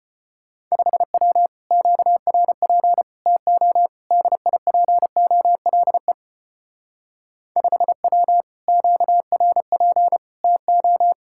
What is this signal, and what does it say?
Morse code, heard as a keyed tone, about 31 wpm, 700 Hz. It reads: 5W QRP TO DIPOLE 5W QRP TO
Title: Speed-Racing Common QSO Phrases - 25wpm